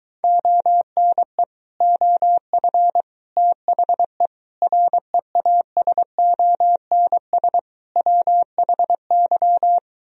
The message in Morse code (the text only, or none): ONE OF THE REASONS WHY